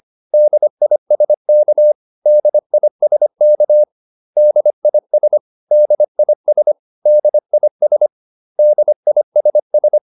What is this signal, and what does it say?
Morse code, heard as a keyed tone, about 25 wpm, 600 Hz. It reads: DISK DISK DIS DIS DIS DISS